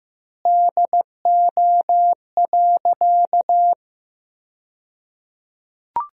Morse code, heard as a keyed tone, about 15 wpm, 700 Hz. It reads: DO.
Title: 2x Word-Spacing - Sentences from Top 100 Words - 15wpm